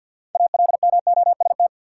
Morse code, heard as a keyed tone, about 52 wpm, 700 Hz. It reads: ALMOST
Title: Six Letter Words - 50wpm